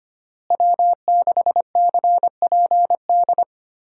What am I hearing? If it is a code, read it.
Morse code, heard as a keyed tone, about 25 wpm, 700 Hz. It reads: W6CPD
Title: USA Call Signs - 5 Characters - Repeated 3x - 25wpm